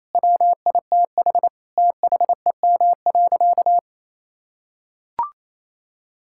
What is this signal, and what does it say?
Morse code, heard as a keyed tone, about 27 wpm, 700 Hz. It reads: WITH THEM.
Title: Sentences from Top 1000 Words - Review Part 5 - 28wpm